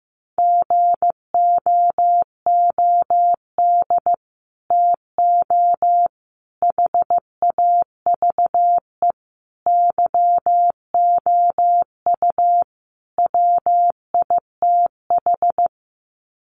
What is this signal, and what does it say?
Morse code, heard as a keyed tone, about 15 wpm, 700 Hz. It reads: GOOD TO HAVE YOU WITH
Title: Sets of 5 Words Encore2 15wpm